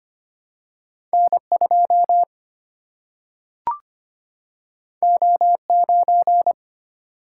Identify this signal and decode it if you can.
Morse code, heard as a keyed tone, about 25 wpm, 700 Hz. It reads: N 2 O 9
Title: Single-Letter Single-Number 25wpm